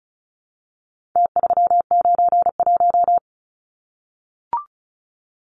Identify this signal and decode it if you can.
Morse code, heard as a keyed tone, about 34 wpm, 700 Hz. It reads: T391
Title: Numbers - Contest Style - 4-Digit 35wpm